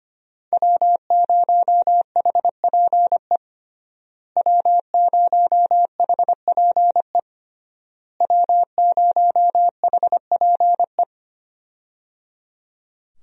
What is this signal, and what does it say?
Morse code, heard as a keyed tone, about 25 wpm, 700 Hz. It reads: W0HPE W0HPE W0HPE
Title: USA Call Signs - 5 Characters - Repeated 3x - 25wpm